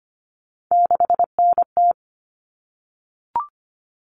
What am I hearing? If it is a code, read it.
Morse code, heard as a keyed tone, about 25 wpm, 700 Hz. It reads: 6NT
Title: All Cut Numbers - 3-Digit 25wpm